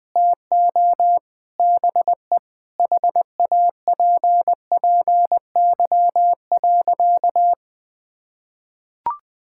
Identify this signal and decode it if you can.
Morse code, heard as a keyed tone, about 20 wpm, 700 Hz. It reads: TO BE HAPPY.
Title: Sentences from Top 1000 Words - Review Part 5 - 20wpm